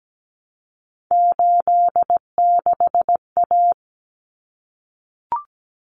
Morse code, heard as a keyed tone, about 17 wpm, 690 Hz. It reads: 86A
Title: Common Cut Numbers - 3-Digit 17wpm